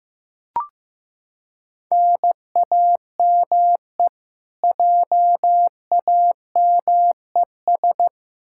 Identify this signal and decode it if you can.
Morse code, heard as a keyed tone, about 15 wpm, 700 Hz. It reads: NAME JAMES
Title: Practice QSO - Exchange 1 - Beginning - 15wpm